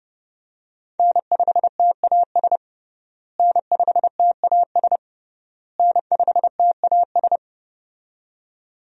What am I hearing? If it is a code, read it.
Morse code, heard as a keyed tone, about 29 wpm, 700 Hz. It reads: N5TAS N5TAS N5TAS
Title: USA Call Signs - 5 Characters - Repeated 3x - 30wpm